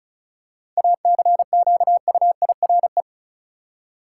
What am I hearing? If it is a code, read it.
Morse code, heard as a keyed tone, about 34 wpm, 700 Hz. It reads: ACQUIRE
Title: Seven Letter Words - 35wpm